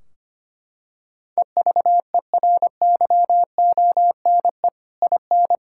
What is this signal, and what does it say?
Morse code, heard as a keyed tone, about 25 wpm, 700 Hz. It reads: EVERYONE IN
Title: Sentences from Top 1600 Words 25wpm